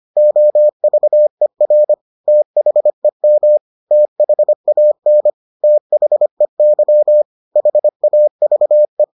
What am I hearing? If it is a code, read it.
Morse code, heard as a keyed tone, about 25 wpm, 600 Hz. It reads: OVER THEM THAN THEY HAVE